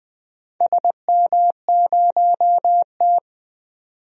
Morse code, heard as a keyed tone, about 21 wpm, 700 Hz. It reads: SM0T
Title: DXCC-100 Call Signs - 4 Characters - 20wpm